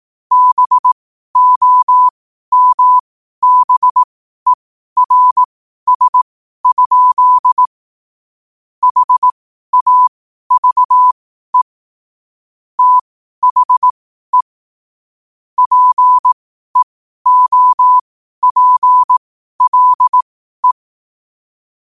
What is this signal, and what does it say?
Morse code, heard as a keyed tone, about 18 wpm, spaced as for 12 wpm, 1000 Hz. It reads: BOMBERS? HAVE THE PEOPLE
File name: qotdfiles / 12wpm